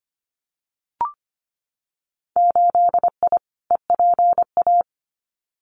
Morse code, heard as a keyed tone, about 25 wpm, 700 Hz. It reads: 8I EPA
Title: Winter Field Day Exchanges Encore 25wpm